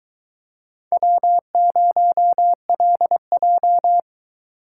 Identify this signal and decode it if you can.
Morse code, heard as a keyed tone, about 22 wpm, 700 Hz. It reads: W0LJ